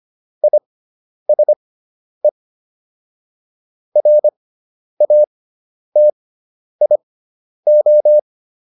Morse code, heard as a keyed tone, about 25 wpm, spaced as for 10 wpm, 600 Hz. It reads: ISE RATIO